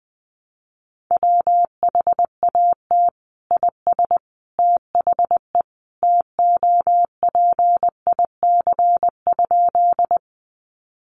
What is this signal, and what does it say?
Morse code, heard as a keyed tone, about 20 wpm, 700 Hz. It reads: WHAT IS THE TOPIC?
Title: Sentences from Top 1200 Words 20wpm